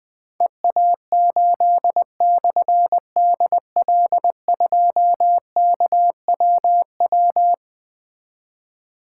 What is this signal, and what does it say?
Morse code, heard as a keyed tone, about 20 wpm, 700 Hz. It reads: EA8/DL2KWW
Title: Difficult Call Signs 20wpm